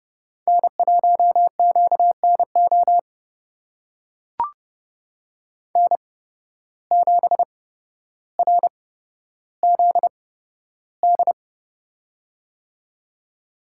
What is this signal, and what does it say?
Morse code, heard as a keyed tone, about 30 wpm, spaced as for 13 wpm, 700 Hz. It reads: N1QNO N7RZD
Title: USA Call Signs - 5 Characters - 8x Character Spacing - 30wpm